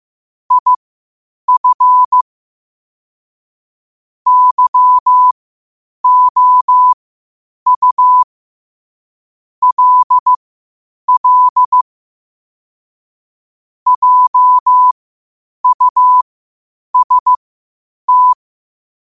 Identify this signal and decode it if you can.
Morse code, heard as a keyed tone, about 15 wpm, spaced as for 8 wpm, 1000 Hz. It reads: IF YOU LL JUST
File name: qotdfiles / 8wpm